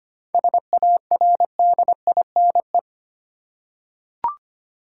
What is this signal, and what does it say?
Morse code, heard as a keyed tone, about 25 wpm, 700 Hz. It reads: SARDINE
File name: practice / 25wpm / Difficult Words with Hints - 25wpm